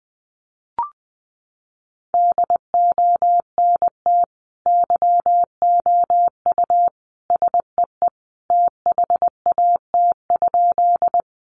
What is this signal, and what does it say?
Morse code, heard as a keyed tone, about 20 wpm, 700 Hz. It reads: DONT YOU SEE THAT?
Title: Sentences from Top 300 Words 20wpm